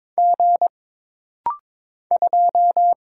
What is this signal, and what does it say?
Morse code, heard as a keyed tone, about 22 wpm, 700 Hz. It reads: G 2
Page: This rapid-fire timing is geared to push your ICR (Instant Character Recognition) skills.